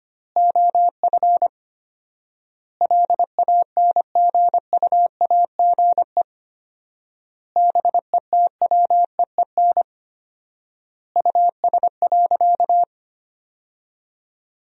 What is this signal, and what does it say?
Morse code, heard as a keyed tone, about 25 wpm, 700 Hz. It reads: OF LANGUAGE BETWEEN US.
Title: 4x Word-Spacing - Sentences from Top 500 Words - 25wpm